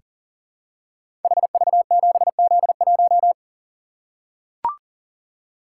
Morse code, heard as a keyed tone, about 38 wpm, 700 Hz. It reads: HV7Z1
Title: Code Groups - 5 Letter:Numbers - 40wpm